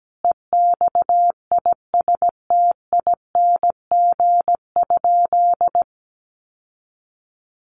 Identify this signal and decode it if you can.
Morse code, heard as a keyed tone, about 17 wpm, 700 Hz. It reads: EXISTING?
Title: Sentences from Top 1500 Words 17wpm